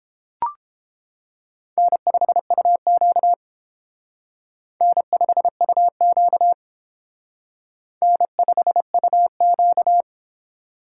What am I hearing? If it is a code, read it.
Morse code, heard as a keyed tone, about 28 wpm, 700 Hz. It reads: N5UQ N5UQ N5UQ